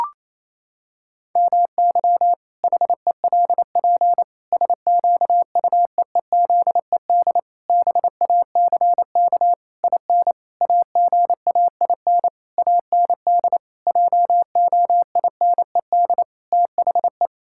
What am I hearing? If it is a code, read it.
Morse code, heard as a keyed tone, about 28 wpm, 700 Hz. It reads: MY HELP SQUEEZED BACK IN AGAIN AND JOINED THE
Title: Pangram Challenge v2 - 28wpm